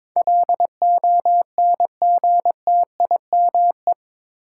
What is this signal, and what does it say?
Morse code, heard as a keyed tone, about 22 wpm, 700 Hz. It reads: LONGTIME
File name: practice / 22wpm / Eight Letter Words - 22wpm